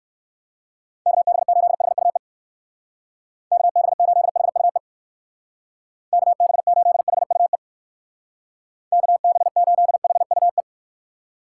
Morse code, heard as a keyed tone, about 59 wpm, 700 Hz. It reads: KB8HUE KB8HUE KB8HUE KB8HUE